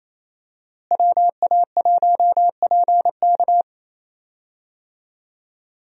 Morse code, heard as a keyed tone, about 28 wpm, 700 Hz. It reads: WA1PK
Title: USA Call Signs - 5 Characters - Repeated 2x - 28wpm